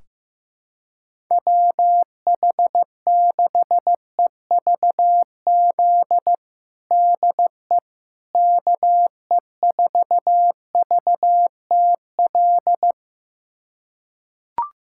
Morse code, heard as a keyed tone, about 16 wpm, 700 Hz. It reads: WH6EVZ DE KE4VTL
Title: USA Callsign DE Callsign - 6 Characters - 15wpm